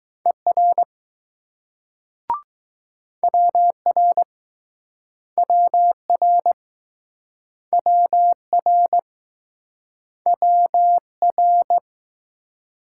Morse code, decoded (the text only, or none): ER WR WR WR WR